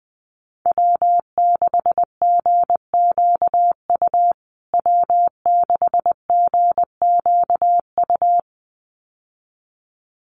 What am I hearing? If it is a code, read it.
Morse code, heard as a keyed tone, about 20 wpm, 700 Hz. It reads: W6GQU W6GQU
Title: USA Call Signs Repeated 2x 20wpm